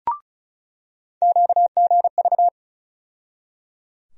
Morse code, heard as a keyed tone, about 36 wpm, 700 Hz. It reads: QGV
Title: USA Call Sign Postfixes - 3 Letters 35wpm